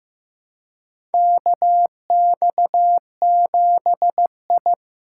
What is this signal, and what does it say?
Morse code, heard as a keyed tone, about 15 wpm, 700 Hz. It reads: KX7I
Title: USA Call Signs - 4 Characters - 4x Character Spacing - 15wpm